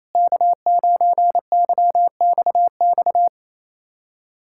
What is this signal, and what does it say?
Morse code, heard as a keyed tone, about 27 wpm, 700 Hz. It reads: K9YXX
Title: USA Callsign DE Callsign - 5 Characters - 28wpm